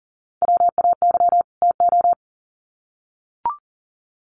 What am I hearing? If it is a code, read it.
Morse code, heard as a keyed tone, about 40 wpm, 700 Hz. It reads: WAY TO
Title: Sets of 2 Words 40wpm